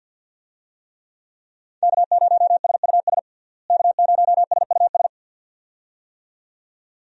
Morse code, heard as a keyed tone, about 50 wpm, 700 Hz. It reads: K0SUS K0SUS
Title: USA Call Signs - 5 Characters - Repeated 2x - 50wpm